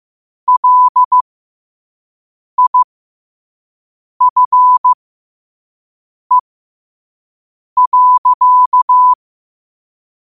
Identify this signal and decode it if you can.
Morse code, heard as a keyed tone, about 15 wpm, spaced as for 5 wpm, 1000 Hz. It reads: LIFE.
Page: Quotes for Fri, 15 Aug 2025 in Morse Code at 5 words per minute.